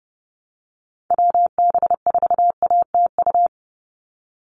Morse code, heard as a keyed tone, about 30 wpm, 700 Hz. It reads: WB4ATU
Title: USA Call Signs 30wpm